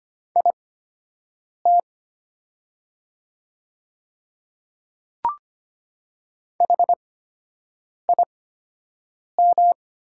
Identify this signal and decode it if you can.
Morse code, heard as a keyed tone, about 25 wpm, spaced as for 7 wpm, 700 Hz. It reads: IT HIM
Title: 8x Character Spacing - Top 100 Words - 25wpm